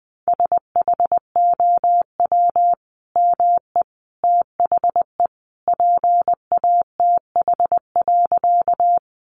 Morse code, heard as a keyed tone, about 20 wpm, 700 Hz. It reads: SHOW ME THE PATH.
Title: Sentences from Top 1400 Words 20wpm